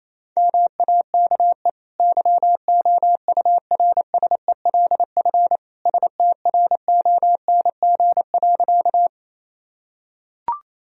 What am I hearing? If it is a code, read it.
Morse code, heard as a keyed tone, about 28 wpm, 700 Hz. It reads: MAKE YOURSELF STRONG.
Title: Sentences from Top 1000 Words - Review Part 3 - 28wpm